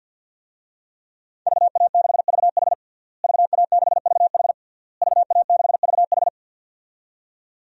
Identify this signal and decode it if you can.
Morse code, heard as a keyed tone, about 52 wpm, 700 Hz. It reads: VA6VH VA6VH VA6VH
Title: DXCC-100 Call Signs - 5 Characters - Repeated 3x - 50wpm